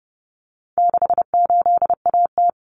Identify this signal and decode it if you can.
Morse code, heard as a keyed tone, about 30 wpm, 700 Hz. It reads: 68AT
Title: Common Cut Numbers - 4-Digit 30wpm